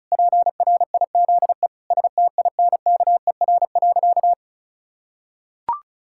Morse code, heard as a keyed tone, about 34 wpm, 700 Hz. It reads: PRIZE STINKER.
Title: Pangram Challenge v2 - 35wpm